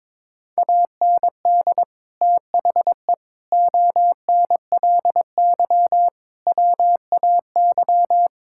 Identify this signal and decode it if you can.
Morse code, heard as a keyed tone, about 22 wpm, 700 Hz. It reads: AND THE ONLY WAY
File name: Sentences with 9-Word Phrases - 22wpm